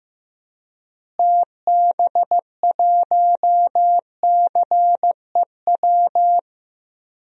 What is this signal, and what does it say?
Morse code, heard as a keyed tone, about 16 wpm, 700 Hz. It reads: TB1CEW
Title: DXCC-100 Call Signs - 6 Characters - Repeated 2x - 15wpm